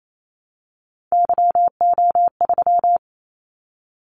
Morse code, heard as a keyed tone, about 28 wpm, 700 Hz. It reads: YO3
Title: Callsign Prefixes for Top 25 DXCC - 28wpm